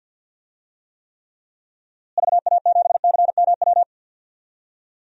Morse code, heard as a keyed tone, about 47 wpm, 700 Hz. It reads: VA7CMW
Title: DXCC-100 Call Signs - 6 Characters - Repeated 3x - 50wpm